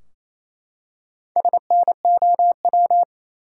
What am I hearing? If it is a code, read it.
Morse code, heard as a keyed tone, about 28 wpm, 700 Hz. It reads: SNOW
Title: Four Letter Words - 28wpm